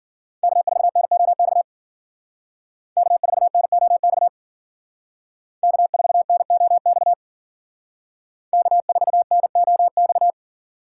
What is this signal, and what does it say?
Morse code, heard as a keyed tone, about 52 wpm, 700 Hz. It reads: K4NOX K4NOX K4NOX K4NOX